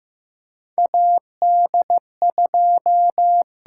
Morse code, heard as a keyed tone, about 15 wpm, 700 Hz. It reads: AD2
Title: USA Call Signs - 4 Characters - 8x Character Spacing - 15wpm